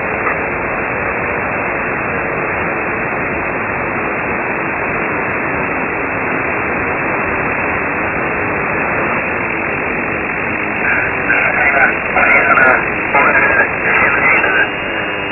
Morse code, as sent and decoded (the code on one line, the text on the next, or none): none